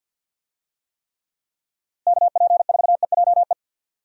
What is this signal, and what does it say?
Morse code, heard as a keyed tone, about 52 wpm, 700 Hz. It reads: KP4EJE
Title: DXCC-100 Call Signs - 6 Characters - Repeated 3x - 50wpm